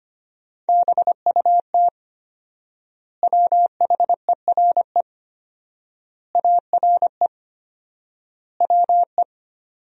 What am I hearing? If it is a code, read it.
Morse code, heard as a keyed tone, about 25 wpm, 700 Hz. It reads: BUT WHERE ARE WE
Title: 4x Word-Spacing - Sentences from Top 200 Words - 25wpm